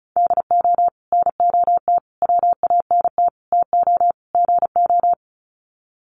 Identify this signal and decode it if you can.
Morse code, heard as a keyed tone, about 34 wpm, 700 Hz. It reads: DO NOT WANT TO GO
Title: Sets of 5 Words 35wpm